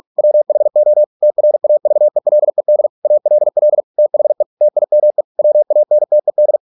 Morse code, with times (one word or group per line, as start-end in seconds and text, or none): 0.2-1.0 WHO
1.2-2.9 TRAVELED
3.0-3.8 ALL
4.0-4.4 THE
4.6-5.2 TIME
5.4-6.6 WANTED